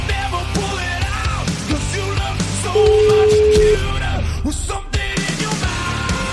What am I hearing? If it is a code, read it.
Ringback tone (Europe, 425 Hz), with noise.